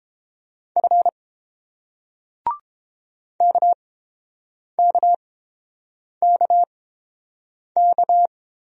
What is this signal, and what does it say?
Morse code, heard as a keyed tone, about 31 wpm, spaced as for 7 wpm, 690 Hz. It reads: F KKKK